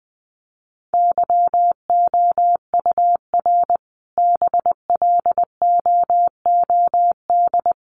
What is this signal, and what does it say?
Morse code, heard as a keyed tone, about 20 wpm, 700 Hz. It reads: YOUR BLOOD
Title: Sentences from Top 1100 Words 20wpm